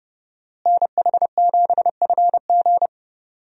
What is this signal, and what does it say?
Morse code, heard as a keyed tone, about 30 wpm, 700 Hz. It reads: NH7FG
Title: USA Call Signs - 5 Characters - Repeated 2x - 30wpm